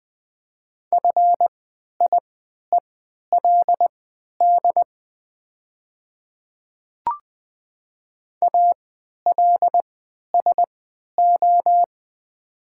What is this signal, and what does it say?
Morse code, heard as a keyed tone, about 20 wpm, spaced as for 11 wpm, 700 Hz. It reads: FIELD ALSO
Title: 3x Character Spacing - Top 500 Words - 20wpm